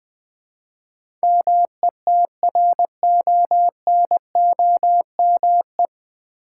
Morse code, heard as a keyed tone, about 20 wpm, 690 Hz. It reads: METRONOME
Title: Difficult Words with Hints - 20wpm